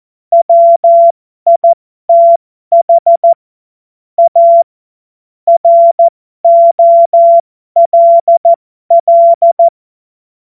Morse code, heard as a keyed tone, about 14 wpm, spaced as for 12 wpm, 680 Hz. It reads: WITH A ROLL
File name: Morsekurs / Witze